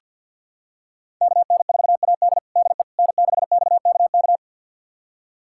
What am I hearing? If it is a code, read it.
Morse code, heard as a keyed tone, about 49 wpm, 700 Hz. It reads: KN4AD DE N6XKK
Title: USA Callsign DE Callsign - 5 Characters - 50wpm